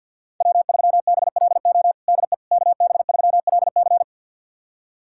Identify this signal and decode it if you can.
Morse code, heard as a keyed tone, about 51 wpm, 700 Hz. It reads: W3BLO DE KB3LC
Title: USA Callsign DE Callsign - 5 Characters - 50wpm